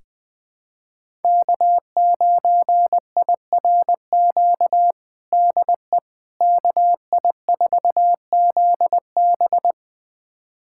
Morse code, heard as a keyed tone, about 20 wpm, 700 Hz. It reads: K9IRQ DE KI4ZB
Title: USA Callsign DE Callsign - 5 Characters - 20wpm